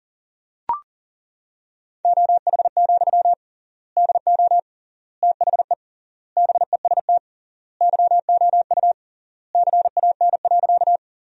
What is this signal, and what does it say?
Morse code, heard as a keyed tone, about 40 wpm, 700 Hz. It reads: OH, DO THE BEST YOU CAN.
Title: 3x Word-Spacing - Sentences from Top 500 Words - 40wpm